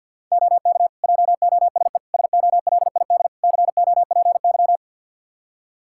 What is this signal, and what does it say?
Morse code, heard as a keyed tone, about 49 wpm, 700 Hz. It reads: OK JOSE SOLID COPY
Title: Practice QSO - Exchange 3 - More Info - 50wpm